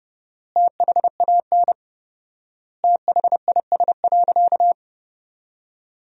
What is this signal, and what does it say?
Morse code, heard as a keyed tone, about 31 wpm, 700 Hz. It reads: THAN THIS.
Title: 4x Word-Spacing - Sentences from Top 100 Words - 30wpm